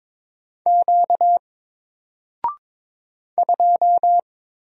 Morse code, heard as a keyed tone, about 23 wpm, 700 Hz. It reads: Q 2
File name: Single-Letter-Number-22wpm